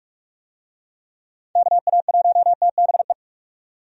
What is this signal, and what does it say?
Morse code, heard as a keyed tone, about 46 wpm, 700 Hz. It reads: KA1TBE
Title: USA Call Signs - 6 Characters - 45wpm